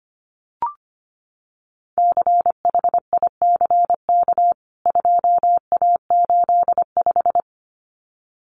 Morse code, heard as a keyed tone, about 25 wpm, 700 Hz. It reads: CHICK 2A85
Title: CWT Exchange 25wpm